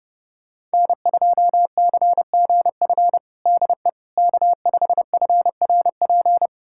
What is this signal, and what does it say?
Morse code, heard as a keyed tone, about 30 wpm, 700 Hz. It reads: N2CGF DE K5FRP
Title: USA Callsign DE Callsign - 5 Characters - 30wpm